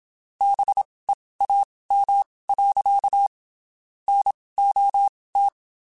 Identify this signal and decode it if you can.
Morse code, heard as a keyed tone, about 27 wpm, spaced as for 19 wpm, 800 Hz. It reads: BEAM. NOT
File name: (spaced) / QSOs Spaced-mp3s / WD0BGP 1 @25 sp